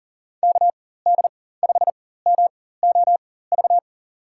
Morse code, heard as a keyed tone, about 40 wpm, spaced as for 19 wpm, 700 Hz. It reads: KD5MOV
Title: USA Call Signs - 4x Character Spacing - 40wpm